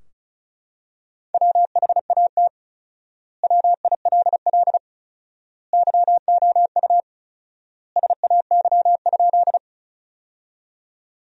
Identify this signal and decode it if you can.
Morse code, heard as a keyed tone, about 36 wpm, 700 Hz. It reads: WHAT WILL YOU SAY?
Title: 4x Word-Spacing - Sentences from Top 200 Words - 35wpm